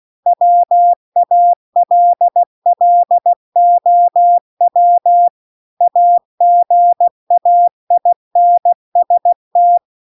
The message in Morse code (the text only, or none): WALLOW AGAINST